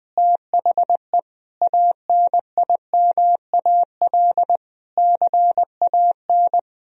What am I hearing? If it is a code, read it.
Morse code, heard as a keyed tone, about 21 wpm, 700 Hz. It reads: THE ANIMAL CAN
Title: Sentences from Top 1000 Words - Review Part 1 - 20wpm